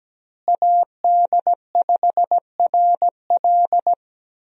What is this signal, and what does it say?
Morse code, heard as a keyed tone, about 17 wpm, 700 Hz. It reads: AD5RL
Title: USA Call Signs - 5 Characters - Repeated 3x - 17wpm